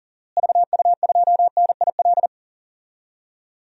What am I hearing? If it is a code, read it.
Morse code, heard as a keyed tone, about 38 wpm, 700 Hz. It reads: VU2NIL
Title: DXCC-100 Call Signs - 6 Characters - Repeated 2x - 40wpm